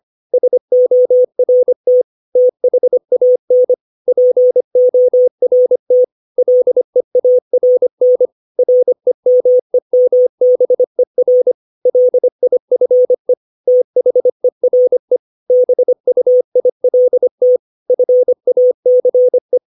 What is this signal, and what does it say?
Morse code, heard as a keyed tone, about 25 wpm, 500 Hz. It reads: SORT THAN PORT LEARN REMEMBER LIFE THERE BUILT FACE
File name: 632 words / 632 Words 25WPM